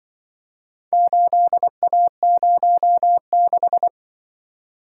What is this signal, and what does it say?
Morse code, heard as a keyed tone, about 24 wpm, 700 Hz. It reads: 8A06
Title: Speed-Racing - Four Letter-Number Combinations - 20wpm